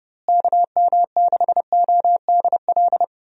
Morse code, heard as a keyed tone, about 30 wpm, 700 Hz. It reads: KM6ODL
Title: USA Call Signs - 6 Characters - Repeated 2x - 30wpm